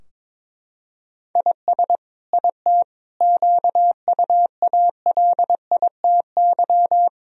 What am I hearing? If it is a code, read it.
Morse code, heard as a keyed tone, about 22 wpm, 700 Hz. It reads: IS IT QUALITY
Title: Sentences from Top 1000 Words - Review Part 4 - 22wpm